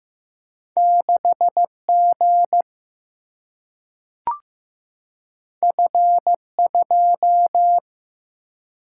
Morse code, heard as a keyed tone, about 15 wpm, 700 Hz. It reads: 6G F2
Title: Single-Letter Single-Number 15wpm